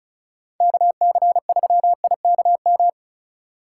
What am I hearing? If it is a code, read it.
Morse code, heard as a keyed tone, about 36 wpm, 700 Hz. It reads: KC3IKM
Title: Speed Racing - USA Call Signs 30wpm